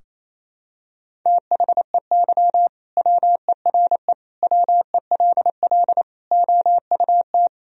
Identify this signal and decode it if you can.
Morse code, heard as a keyed tone, about 28 wpm, 700 Hz. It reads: THEY WERE WELL OUT
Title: Sentences from Top 1000 Words - Review Part 3 - 28wpm